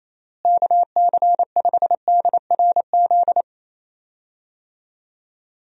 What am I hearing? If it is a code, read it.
Morse code, heard as a keyed tone, about 28 wpm, 700 Hz. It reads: KC5DRZ
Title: USA Call Signs - 6 Characters - Repeated 2x - 28wpm